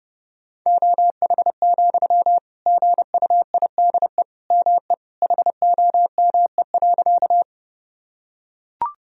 Morse code, heard as a keyed tone, about 32 wpm, 700 Hz. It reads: OH, GUIDE ME HOME.
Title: Sentences from Top 1000 Words - Review Part 3 - 30wpm